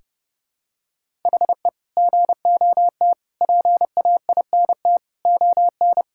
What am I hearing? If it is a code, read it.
Morse code, heard as a keyed tone, about 32 wpm, 700 Hz. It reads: HE GOT PAINT ON
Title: Sentences from Top 1000 Words - Review Part 6 - 30wpm